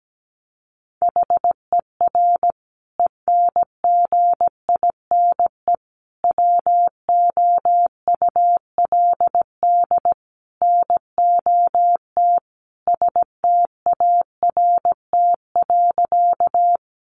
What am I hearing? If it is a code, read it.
Morse code, heard as a keyed tone, about 17 wpm, 700 Hz. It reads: HER ENGINE WOULD NOT START.
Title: Sentences from Top 1500 Words 17wpm